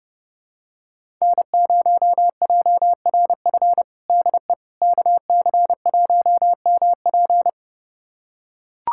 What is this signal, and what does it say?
Morse code, heard as a keyed tone, about 30 wpm, 700 Hz. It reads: N0JRF DE KC1MP
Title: USA Callsign DE Callsign - 5 Characters - 30wpm